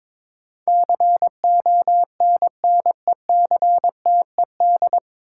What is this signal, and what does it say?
Morse code, heard as a keyed tone, about 21 wpm, 700 Hz. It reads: CONNECTED
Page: Morse Code Ninja practice 22wpm